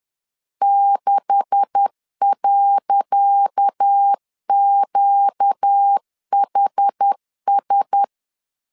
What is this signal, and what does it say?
Morse code, heard as a keyed tone, about 10 wpm, 800 Hz. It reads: 6.QHS
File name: LEZIONI / 10 SIGNS / 10C DOT 10wpm 2940